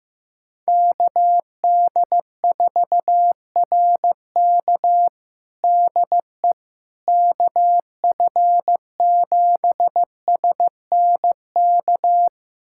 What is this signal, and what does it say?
Morse code, heard as a keyed tone, about 16 wpm, 700 Hz. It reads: KD4RK DE KF7SNK
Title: USA Callsign DE Callsign 15wpm